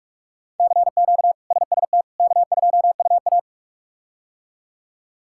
Morse code, heard as a keyed tone, about 42 wpm, 700 Hz. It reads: CQ SST K2UA
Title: K1USN SST QSO Practice - 45wpm